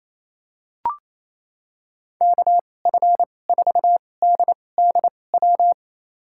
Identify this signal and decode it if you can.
Morse code, heard as a keyed tone, about 28 wpm, spaced as for 20 wpm, 700 Hz. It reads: KF4DDW
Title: USA Call Signs - 6 Characters - 2x Character Spacing - 28wpm